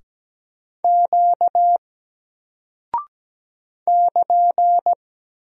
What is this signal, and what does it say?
Morse code, heard as a keyed tone, about 17 wpm, 700 Hz. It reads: Q <KN>
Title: Single Character (Letter:Number:Punctuation:Prosign) 17wpm